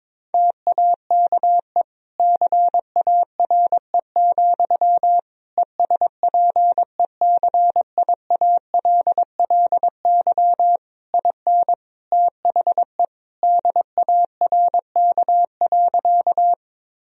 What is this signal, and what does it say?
Morse code, heard as a keyed tone, about 22 wpm, 700 Hz. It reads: TAKE CARE, ESPECIALLY IN THE DARK.
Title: Sentences from Top 1000 Words - Review Part 4 - 22wpm